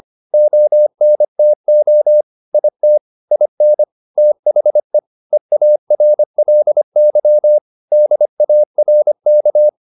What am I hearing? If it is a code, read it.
Morse code, heard as a keyed tone, about 25 wpm, 600 Hz. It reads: ONTO IT IN THE EARLY DARK